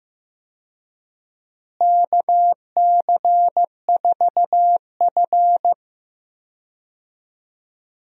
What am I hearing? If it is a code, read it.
Morse code, heard as a keyed tone, about 15 wpm, 700 Hz. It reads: KC4F
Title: USA Call Signs - 4 Characters - Repeated 2x - 15wpm